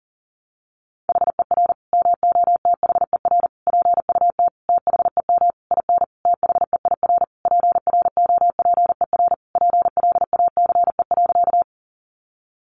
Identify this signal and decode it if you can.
Morse code, heard as a keyed tone, about 40 wpm, 700 Hz. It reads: HER MOTHER PUT THEM IN THEIR PROPER PLACE.
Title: Sentences from Top 800 Words 40wpm